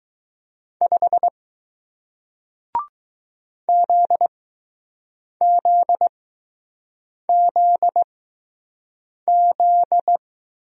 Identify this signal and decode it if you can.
Morse code, heard as a keyed tone, about 20 wpm, spaced as for 6 wpm, 700 Hz. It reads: 5 ZZZZ